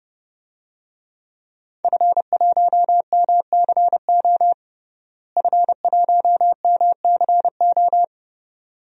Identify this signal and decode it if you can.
Morse code, heard as a keyed tone, about 30 wpm, 700 Hz. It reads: F1MCO F1MCO
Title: DXCC-100 Call Signs - 5 Characters - Repeated 2x - 30wpm